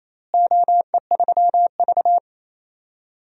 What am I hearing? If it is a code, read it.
Morse code, heard as a keyed tone, about 27 wpm, 700 Hz. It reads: OE3V
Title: DXCC-100 Call Signs - 4 Characters - Repeated 3x - 28wpm